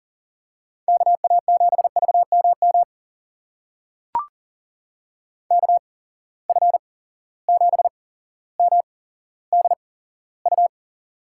Morse code, heard as a keyed tone, about 40 wpm, spaced as for 17 wpm, 700 Hz. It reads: KA7VMM KF7MDU
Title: USA Call Signs - 6 Characters - 8x Character Spacing - 40wpm